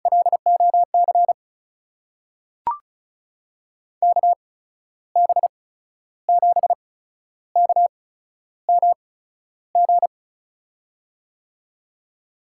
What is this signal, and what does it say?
Morse code, heard as a keyed tone, about 36 wpm, spaced as for 12 wpm, 700 Hz. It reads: LOC KB7KMG
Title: USA Call Signs - 6 Characters - 8x Character Spacing - 35wpm